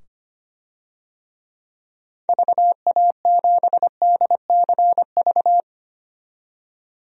Morse code, heard as a keyed tone, about 25 wpm, 700 Hz. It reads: VA7DCV
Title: DXCC-100 Call Signs - 6 Characters - 25wpm